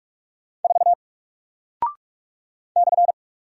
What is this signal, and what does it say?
Morse code, heard as a keyed tone, about 44 wpm, 700 Hz. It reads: 4 /
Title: Single Character (Letter:Number:Punctuation:Prosign) Alt 45wpm